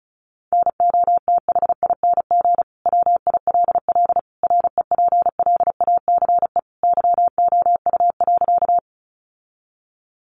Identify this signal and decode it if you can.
Morse code, heard as a keyed tone, about 34 wpm, 700 Hz. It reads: NOTHING WILL REPLACE YOU.
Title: Sentences from Top 1400 Words 35wpm